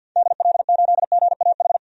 Morse code, heard as a keyed tone, about 47 wpm, 700 Hz. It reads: DL8GAH
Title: DXCC-100 Call Signs - 6 Characters - 50wpm